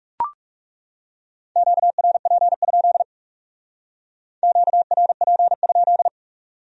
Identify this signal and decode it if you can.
Morse code, heard as a keyed tone, about 38 wpm, 700 Hz. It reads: QRP? QRP?